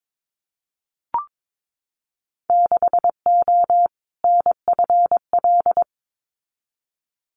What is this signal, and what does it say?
Morse code, heard as a keyed tone, about 22 wpm, 700 Hz. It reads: 6O NFL
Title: Winter Field Day Exchanges 22wpm